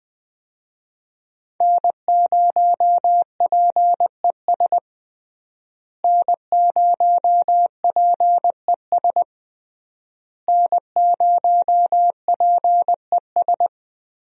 Morse code, heard as a keyed tone, about 20 wpm, 700 Hz. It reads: N0PES N0PES N0PES
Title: USA Call Signs - 5 Characters - Repeated 3x - 20wpm